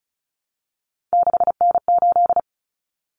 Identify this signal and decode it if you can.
Morse code, heard as a keyed tone, about 34 wpm, 700 Hz. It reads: 6N8
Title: Common Cut Numbers - 3-Digit 35wpm